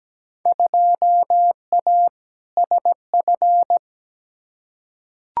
Morse code, heard as a keyed tone, about 17 wpm, 700 Hz. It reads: 2A SF
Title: Field Day Exchanges - 17wpm